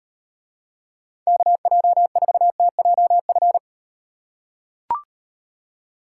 Morse code, heard as a keyed tone, about 38 wpm, 700 Hz. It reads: KJ4TJF